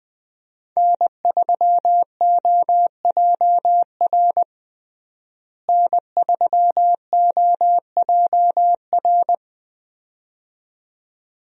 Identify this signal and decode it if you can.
Morse code, heard as a keyed tone, about 20 wpm, 700 Hz. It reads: N3OJR N3OJR
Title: USA Call Signs - 5 Characters - Repeated 2x - 20wpm